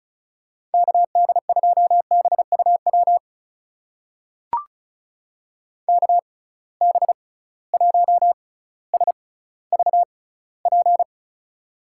Morse code, heard as a keyed tone, about 36 wpm, spaced as for 18 wpm, 700 Hz. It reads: KD2BUW KB1SVP
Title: USA Call Signs - 6 Characters - 6x Character Spacing - 35wpm